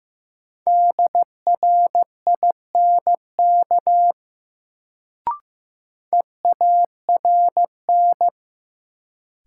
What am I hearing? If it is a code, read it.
Morse code, heard as a keyed tone, about 15 wpm, 700 Hz. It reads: DRINK EARN
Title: Sentences from Top 1500 Words - New Word Drill - 15wpm